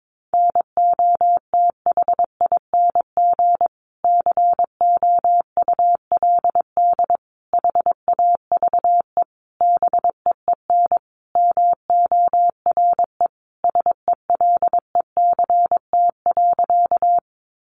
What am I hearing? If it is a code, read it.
Morse code, heard as a keyed tone, about 22 wpm, 700 Hz. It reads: NOTHING COULD HAVE BEEN MORE SELECT.
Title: Sentences from Top 1300 Words 22wpm